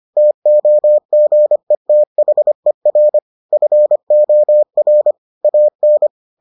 Morse code, heard as a keyed tone, about 25 wpm, 600 Hz. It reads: TOGETHER FOR AN